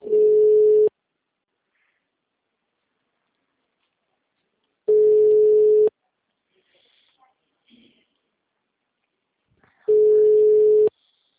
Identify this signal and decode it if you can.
Ringback tone (Europe, 425 Hz), with speech.
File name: out- -2012- - - .63